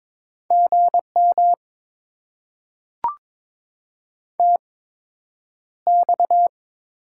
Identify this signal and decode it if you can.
Morse code, heard as a keyed tone, about 22 wpm, spaced as for 9 wpm, 700 Hz. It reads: GM TX
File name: Top 90 QSO Elements - 8x Character Spacing - 22wpm